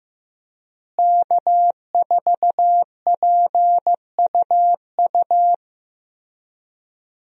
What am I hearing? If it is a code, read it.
Morse code, heard as a keyed tone, about 16 wpm, 700 Hz. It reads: K4PUU
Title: USA Call Signs - 5 Characters - Repeated 3x - 15wpm